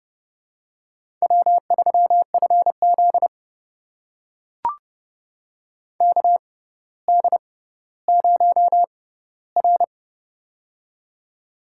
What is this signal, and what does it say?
Morse code, heard as a keyed tone, about 30 wpm, spaced as for 15 wpm, 700 Hz. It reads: W3FZ KD0R
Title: USA Call Signs - 4 Characters - 6x Character Spacing - 30wpm